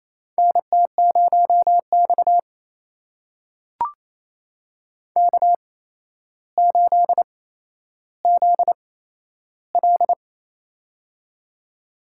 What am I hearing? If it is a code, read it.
Morse code, heard as a keyed tone, about 28 wpm, spaced as for 12 wpm, 700 Hz. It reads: NT0X K8ZL
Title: USA Call Signs - 4 Characters - 8x Character Spacing - 28wpm